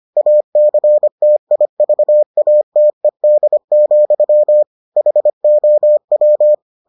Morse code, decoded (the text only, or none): ACTIVATED, HOW